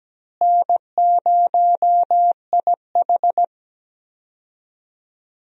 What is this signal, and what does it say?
Morse code, heard as a keyed tone, about 17 wpm, 700 Hz. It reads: N0IH
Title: USA Call Signs - 4 Characters - Repeated 2x - 17wpm